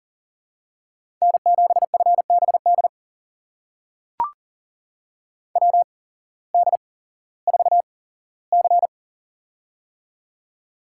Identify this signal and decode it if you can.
Morse code, heard as a keyed tone, about 40 wpm, spaced as for 19 wpm, 700 Hz. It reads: N7FBD WD4C
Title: USA Call Signs - 8x Character Spacing - 40wpm